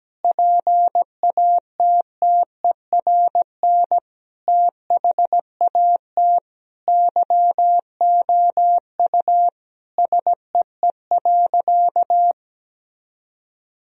Morse code, heard as a keyed tone, about 17 wpm, 700 Hz. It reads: PATTERN THAT YOU SEE.
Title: Sentences from Top 1000 Words - Review Part 2 - 17wpm